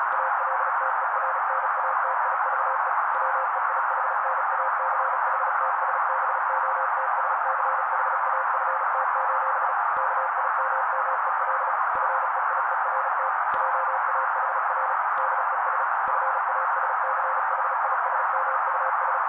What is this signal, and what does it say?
Morse code, heard as a keyed tone, about 35 wpm, 530 Hz. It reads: ALTERNATIVE WEIHNACHTSBOTSCHAFT: KIMMEL FEIERT JAHR DES FASCHISMUS
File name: daily / 35wpm noise